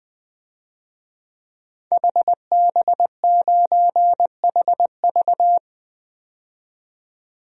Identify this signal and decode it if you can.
Morse code, heard as a keyed tone, about 20 wpm, 700 Hz. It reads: HB9HV
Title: DXCC-100 Call Signs - 5 Characters - 20wpm